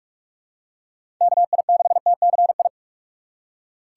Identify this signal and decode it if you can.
Morse code, heard as a keyed tone, about 46 wpm, 700 Hz. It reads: KI6TCI
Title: USA Call Signs - 6 Characters - 2x Character Spacing - 45wpm